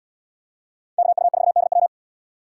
Morse code, heard as a keyed tone, about 72 wpm, 700 Hz. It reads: KU4DW